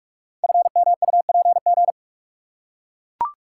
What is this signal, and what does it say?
Morse code, heard as a keyed tone, about 44 wpm, 690 Hz. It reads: FMUPG
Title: Code Groups - 5 Letters - 45wpm